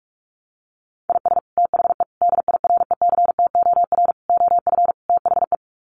Morse code, heard as a keyed tone, about 44 wpm, 700 Hz. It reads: IS THE DIRECTOR OF THE
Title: Sets of 5 Words Encore2 45wpm